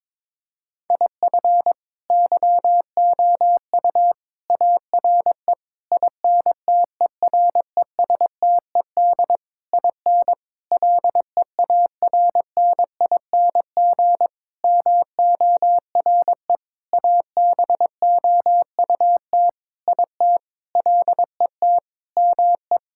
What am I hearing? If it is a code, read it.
Morse code, heard as a keyed tone, about 22 wpm, 700 Hz. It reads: IF YOU ARE INTERESTED IN LEARNING MORE ABOUT IT LET ME
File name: practice / 22wpm / Sentences with 8-Word Phrases - 22wpm